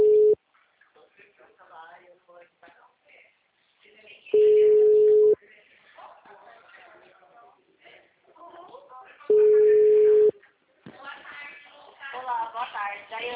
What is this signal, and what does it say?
Ringback tone (Europe, 425 Hz), with speech.